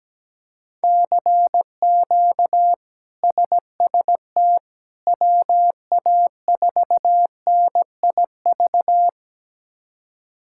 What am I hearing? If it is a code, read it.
Morse code, heard as a keyed tone, about 17 wpm, 700 Hz. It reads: CQ SST WA4NIV
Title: K1USN SST QSO Practice - 17wpm